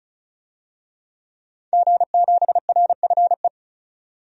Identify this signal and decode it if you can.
Morse code, heard as a keyed tone, about 36 wpm, 700 Hz. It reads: G7RFE
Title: DXCC-100 Call Signs - 5 Characters - 35wpm